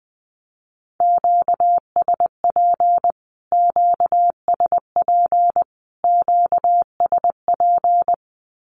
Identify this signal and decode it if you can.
Morse code, heard as a keyed tone, about 20 wpm, 700 Hz. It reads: QSP QSP QSP
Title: Q-Codes Repeated 3x 20wpm